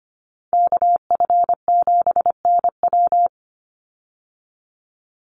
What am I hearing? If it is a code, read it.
Morse code, heard as a keyed tone, about 25 wpm, 700 Hz. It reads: KF7NW
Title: USA Call Signs 25wpm